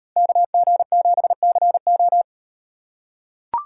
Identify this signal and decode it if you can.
Morse code, heard as a keyed tone, about 38 wpm, 700 Hz. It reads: KG7CO